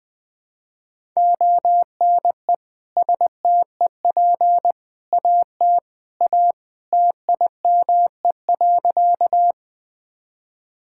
Morse code, heard as a keyed tone, about 19 wpm, 700 Hz. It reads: ONE STEP AT A TIME.
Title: Sentences with 3-Word Phrases - 20wpm